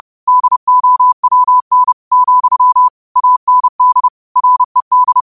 Morse code, heard as a keyed tone, about 30 wpm, 1000 Hz. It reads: DOWN, AND RED